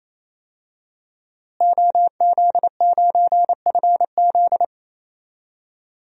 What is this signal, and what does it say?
Morse code, heard as a keyed tone, about 28 wpm, 700 Hz. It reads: OZ9FZ
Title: DXCC-100 Call Signs - 5 Characters - Repeated 3x - 28wpm